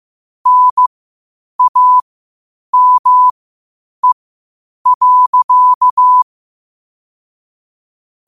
Morse code, heard as a keyed tone, about 15 wpm, spaced as for 8 wpm, 1000 Hz. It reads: NAME.
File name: qotdfiles / 8wpm